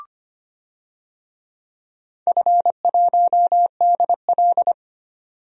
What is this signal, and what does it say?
Morse code, heard as a keyed tone, about 25 wpm, 690 Hz. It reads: F1DL
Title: DXCC-100 Call Signs - 4 Characters - 25wpm